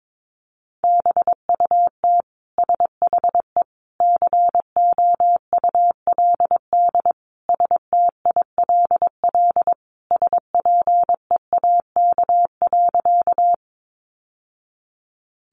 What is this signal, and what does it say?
Morse code, heard as a keyed tone, about 22 wpm, 700 Hz. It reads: BUT SHE COULD STILL SPEAK.
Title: Sentences from Top 1100 Words 22wpm